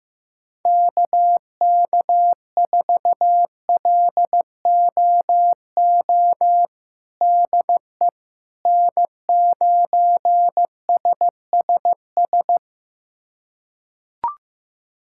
Morse code, heard as a keyed tone, about 15 wpm, 700 Hz. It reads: KK4LOO DE N9SSS
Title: USA Callsign DE Callsign 15wpm